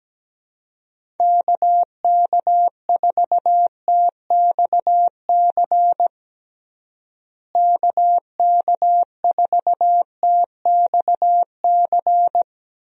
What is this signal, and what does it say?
Morse code, heard as a keyed tone, about 17 wpm, 700 Hz. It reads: KK4TXC KK4TXC
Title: USA Call Signs - 6 Characters - Repeated 3x - 17wpm